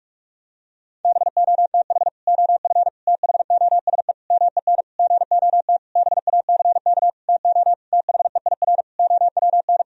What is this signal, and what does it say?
Morse code, heard as a keyed tone, about 46 wpm, 700 Hz. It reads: BOTH OF THOSE MEN GOT BACK TO THEIR OWN
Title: Sentences from Top 1000 Words - Review Part 1 - 45wpm